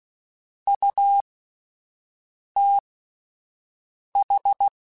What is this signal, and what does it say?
Morse code, heard as a keyed tone, about 16 wpm, spaced as for 5 wpm, 800 Hz. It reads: UTH